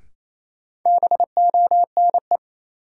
Morse code, heard as a keyed tone, about 28 wpm, 700 Hz. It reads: BONE
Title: Sentences from Top 1600 Words - New Word Drill - 28wpm